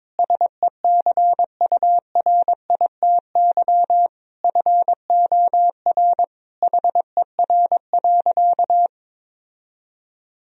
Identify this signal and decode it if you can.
Morse code, heard as a keyed tone, about 22 wpm, 700 Hz. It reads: SECURITY FOR HER.
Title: Sentences from Top 1000 Words - Review Part 4 - 22wpm